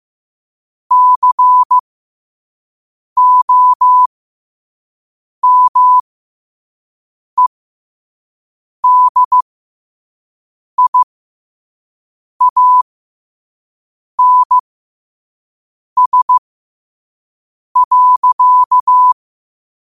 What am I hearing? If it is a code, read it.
Morse code, heard as a keyed tone, about 15 wpm, spaced as for 5 wpm, 1000 Hz. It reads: COMEDIANS.
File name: 5wpm